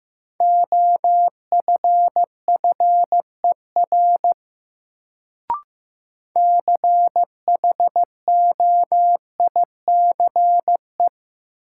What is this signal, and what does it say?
Morse code, heard as a keyed tone, about 15 wpm, 700 Hz. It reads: OFFER CHOICE
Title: Sentences from Top 900 Words - New Word Drill - 15wpm